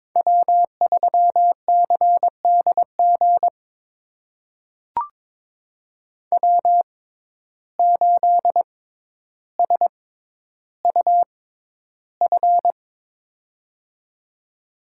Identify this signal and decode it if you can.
Morse code, heard as a keyed tone, about 22 wpm, spaced as for 11 wpm, 700 Hz. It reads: W3CDG W8SUF
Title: USA Call Signs - 5 Characters - 6x Character Spacing - 22wpm